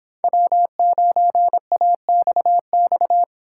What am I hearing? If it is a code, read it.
Morse code, heard as a keyed tone, about 26 wpm, 700 Hz. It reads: W9AXX